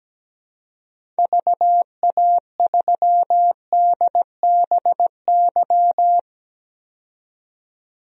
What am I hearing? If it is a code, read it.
Morse code, heard as a keyed tone, about 17 wpm, 700 Hz. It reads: VA3DBY
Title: DXCC-100 Call Signs - 6 Characters - 17wpm